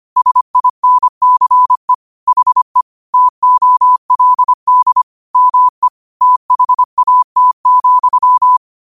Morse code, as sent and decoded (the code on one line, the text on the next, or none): ... .. -. -.-. / .... / - --- .-.. -.. / -- . / - .... .- - --..--
SINCE HE TOLD ME THAT,